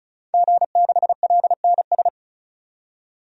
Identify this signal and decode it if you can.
Morse code, heard as a keyed tone, about 36 wpm, 700 Hz. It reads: G6LNS
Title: DXCC-100 Call Signs - 5 Characters - 35wpm